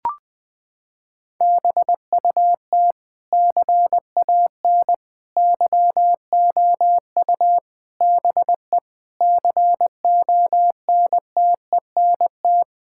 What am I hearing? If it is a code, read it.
Morse code, heard as a keyed tone, about 21 wpm, 700 Hz. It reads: BUT CAN YOU BE CONTENT
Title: Sentences from Top 1000 Words - Review Part 6 - 20wpm